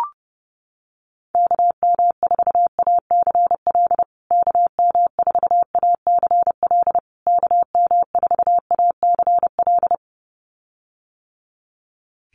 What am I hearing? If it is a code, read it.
Morse code, heard as a keyed tone, about 28 wpm, 700 Hz. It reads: KM4ACL KM4ACL KM4ACL
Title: USA Call Signs Repeated 3x 30wpm